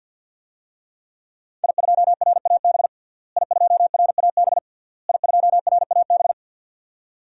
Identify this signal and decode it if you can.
Morse code, heard as a keyed tone, about 46 wpm, 700 Hz. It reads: I2RAB I2RAB I2RAB
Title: DXCC-100 Call Signs - 5 Characters - Repeated 3x - 50wpm